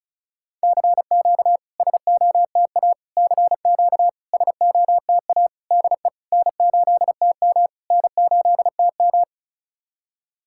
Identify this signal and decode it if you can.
Morse code, heard as a keyed tone, about 34 wpm, 690 Hz. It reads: CQ SOTA CQ SOTA DE N8TM N8TM
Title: SOTA QSO Practice 35wpm